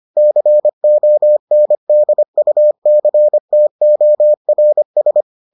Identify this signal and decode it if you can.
Morse code, heard as a keyed tone, about 25 wpm, 600 Hz. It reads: CONDUCTORS